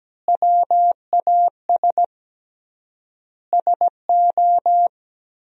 Morse code, heard as a keyed tone, about 17 wpm, 700 Hz. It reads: WAS SO
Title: 3x Word-Spacing - Sentences from Top 300 Words - 17wpm